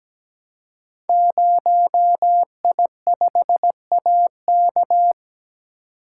Morse code, heard as a keyed tone, about 17 wpm, 700 Hz. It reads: 0I5AK
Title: Code Groups - 5 Letter:Numbers - 17wpm